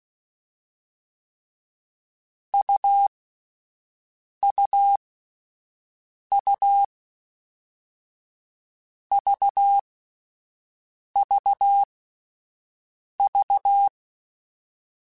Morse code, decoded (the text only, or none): U U U V V V